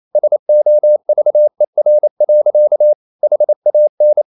Morse code, heard as a keyed tone, about 28 wpm, 600 Hz. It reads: SOVER. HAN
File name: Ferie QRQ 28wpm